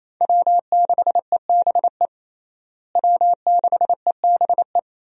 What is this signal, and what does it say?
Morse code, heard as a keyed tone, about 27 wpm, 700 Hz. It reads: W6EBE W6EBE
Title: USA Call Signs - 5 Characters - Repeated 2x - 28wpm